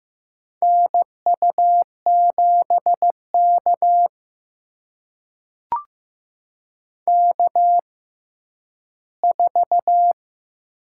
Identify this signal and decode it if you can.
Morse code, heard as a keyed tone, about 15 wpm, spaced as for 10 wpm, 700 Hz. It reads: NU7K K4
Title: USA Call Signs - 4 Characters - 6x Character Spacing - 15wpm